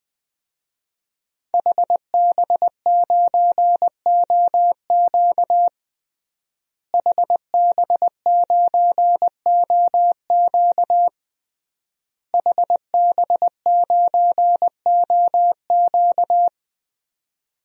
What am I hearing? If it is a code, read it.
Morse code, heard as a keyed tone, about 20 wpm, 700 Hz. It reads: HB9OQ HB9OQ HB9OQ
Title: DXCC-100 Call Signs - 5 Characters - Repeated 3x - 20wpm